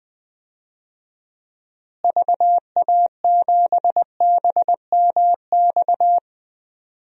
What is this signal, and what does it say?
Morse code, heard as a keyed tone, about 20 wpm, 700 Hz. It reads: VA7BMX
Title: DXCC-100 Call Signs - 6 Characters - Repeated 3x - 20wpm